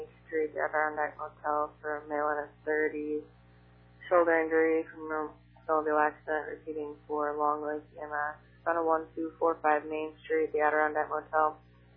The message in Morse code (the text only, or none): none